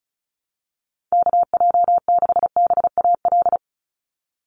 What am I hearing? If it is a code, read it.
Morse code, heard as a keyed tone, about 36 wpm, 700 Hz. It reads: KJ6BAL
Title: USA Call Signs 35wpm